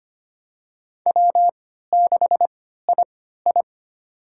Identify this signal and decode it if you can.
Morse code, heard as a keyed tone, about 25 wpm, spaced as for 14 wpm, 700 Hz. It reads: W6II
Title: USA Call Signs - 4 Characters - 3x Character Spacing - 25wpm